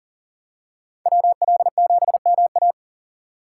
Morse code, heard as a keyed tone, about 40 wpm, 700 Hz. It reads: WL7MA
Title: USA Call Signs - 3x Character Spacing - 40wpm